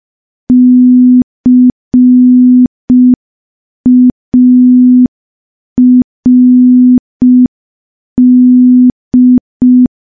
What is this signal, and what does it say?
Morse code, heard as a keyed tone, about 5 wpm, 250 Hz. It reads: CARD
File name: morse / morse-training-05WPM